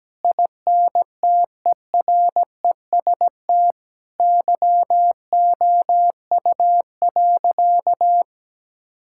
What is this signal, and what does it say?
Morse code, heard as a keyed tone, about 17 wpm, 700 Hz. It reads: INTEREST YOU.
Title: Sentences from Top 1000 Words - Review Part 5 - 17wpm